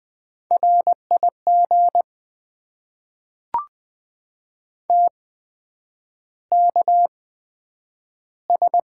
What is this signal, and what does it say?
Morse code, heard as a keyed tone, about 20 wpm, spaced as for 9 wpm, 700 Hz. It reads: RIG TKS
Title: Top 90 QSO Elements - 8x Character Spacing - 20wpm